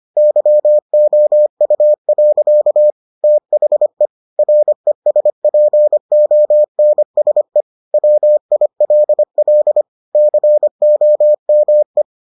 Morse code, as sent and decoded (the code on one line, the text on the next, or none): -.-- --- ..- .-.-.- / - .... / .-. ... .--. --- -. ... / .-- .. .-.. .-.. / -.-. --- -- .
YOU. THE RESPONSE WILL COME